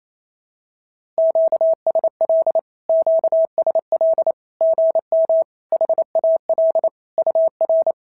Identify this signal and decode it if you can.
Morse code, heard as a keyed tone, about 28 wpm, 650 Hz. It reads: QSL QSL GM HAL UR
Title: SOTA QSO Practice 28wpm